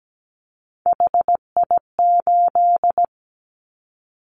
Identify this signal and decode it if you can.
Morse code, heard as a keyed tone, about 17 wpm, 700 Hz. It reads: HI8
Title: Callsign Prefixes for Top 100 DXCC - 17wpm